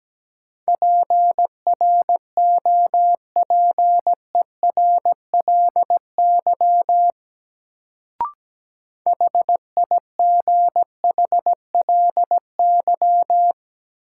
Morse code, heard as a keyed tone, about 17 wpm, 700 Hz. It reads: PROPERLY HIGHLY
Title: Sentences from Top 1100 Words - New Word Drill Alt - 17wpm